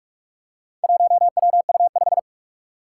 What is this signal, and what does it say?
Morse code, heard as a keyed tone, about 46 wpm, 700 Hz. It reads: 1WU5
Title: Four Challenging Letter-Number Combinations 45wpm